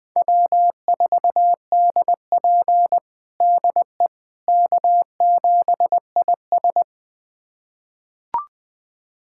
Morse code, heard as a keyed tone, about 20 wpm, 700 Hz. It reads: W4DP DE K7IS
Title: USA Callsign DE Callsign - 4 Characters - 20wpm